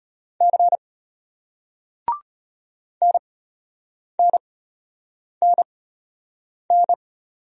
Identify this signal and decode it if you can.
Morse code, heard as a keyed tone, about 35 wpm, spaced as for 8 wpm, 700 Hz. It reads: C NNNN